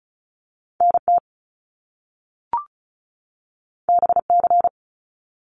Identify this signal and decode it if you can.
Morse code, heard as a keyed tone, about 36 wpm, 700 Hz. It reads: NT BC
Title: Canadian Provinces and Territories 35wpm